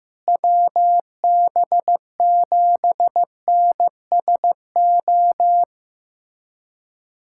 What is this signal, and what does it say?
Morse code, heard as a keyed tone, about 15 wpm, 700 Hz. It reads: WB7NSO
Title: USA Callsign DE Callsign 15wpm